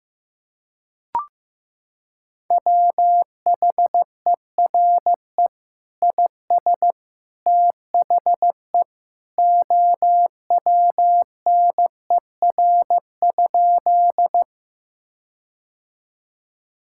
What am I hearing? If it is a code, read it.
Morse code, heard as a keyed tone, about 15 wpm, 700 Hz. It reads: WHERE IS THE OWNER?
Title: Sentences from Top 1600 Words 15wpm